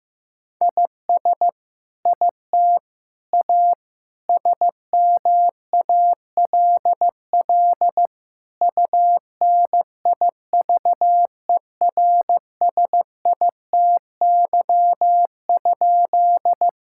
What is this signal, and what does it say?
Morse code, heard as a keyed tone, about 15 wpm, 700 Hz. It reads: IS IT A SMALL UNIVERSITY?
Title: Sentences from Top 1000 Words - Review Part 5 - 15wpm